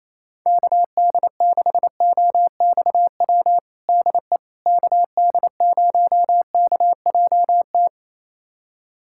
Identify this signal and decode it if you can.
Morse code, heard as a keyed tone, about 28 wpm, 700 Hz. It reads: KD6OXW DE KD0KJT
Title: USA Callsign DE Callsign - 6 Characters - 28wpm